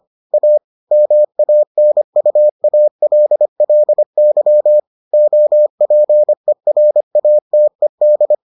Morse code, heard as a keyed tone, about 25 wpm, 600 Hz. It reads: A MANUALLY OPERATED